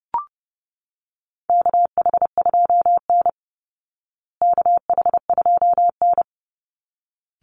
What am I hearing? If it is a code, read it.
Morse code, heard as a keyed tone, about 28 wpm, 700 Hz. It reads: KH2N KH2N
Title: Call Signs Repeated 3x - 30wpm